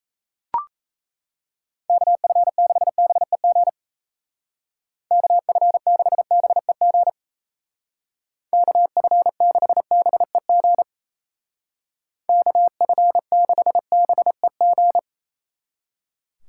Morse code, heard as a keyed tone, about 34 wpm, 700 Hz. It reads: KF6BEG KF6BEG KF6BEG KF6BEG
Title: Speed Racing - USA 2x3 Call Signs 28wpm